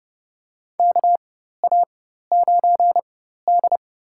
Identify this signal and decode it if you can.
Morse code, heard as a keyed tone, about 30 wpm, spaced as for 14 wpm, 700 Hz. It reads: KA9D
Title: USA Call Signs - 4 Characters - 4x Character Spacing - 30wpm